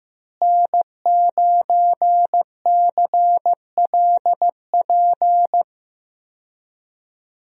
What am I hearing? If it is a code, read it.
Morse code, heard as a keyed tone, about 16 wpm, 700 Hz. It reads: N9CLP
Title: USA Call Signs - 5 Characters - Repeated 2x - 15wpm